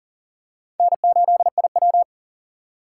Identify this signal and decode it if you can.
Morse code, heard as a keyed tone, about 40 wpm, 700 Hz. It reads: N8IW
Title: USA Call Signs - 4 Characters - 3x Character Spacing - 40wpm